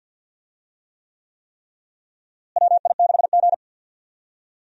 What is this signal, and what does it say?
Morse code, heard as a keyed tone, about 52 wpm, 700 Hz. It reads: WI6G
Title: USA Call Signs - 4 Characters - Repeated 3x - 50wpm